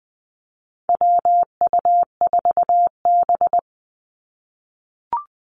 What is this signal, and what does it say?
Morse code, heard as a keyed tone, about 20 wpm, 700 Hz. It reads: WU4B
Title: Call Signs - 20wpm